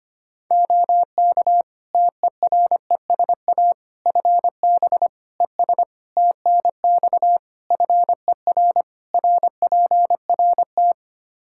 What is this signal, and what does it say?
Morse code, heard as a keyed tone, about 25 wpm, 700 Hz. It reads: OK TERESA FB ES TNX FER RPRT
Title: Practice QSO - Exchange 2 - Info - 25wpm